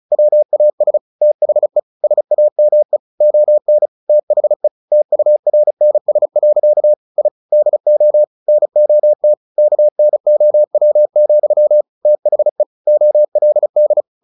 Morse code, heard as a keyed tone, about 34 wpm, 600 Hz. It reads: WAS THE SAME ON THE TURNS. I DO NOT KNOW, THE OLD
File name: the old man and the sea 2 35wpm